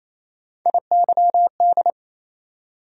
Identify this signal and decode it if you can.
Morse code, heard as a keyed tone, about 28 wpm, 700 Hz. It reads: IYD
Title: USA Call Sign Postfixes - 3 Letters 28wpm